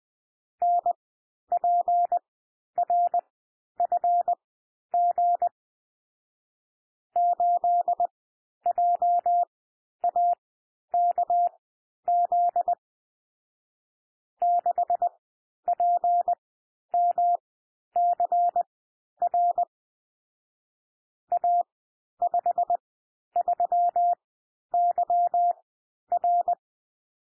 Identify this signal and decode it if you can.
Morse code, heard as a keyed tone, about 20 wpm, spaced as for 11 wpm, 700 Hz. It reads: NPRFG 8JAKZ 6PMCR A53YR
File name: cwmp3 / M1 SOUND MIXED 20 wpm